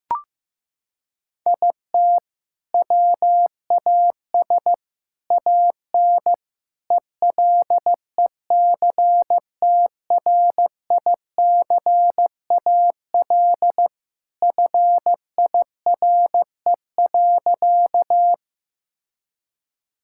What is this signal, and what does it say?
Morse code, heard as a keyed tone, about 14 wpm, 700 Hz. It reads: IT WAS AN ELECTRICAL FIRE.
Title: Sentences from Top 1000 Words - Review Part 6 - 15wpm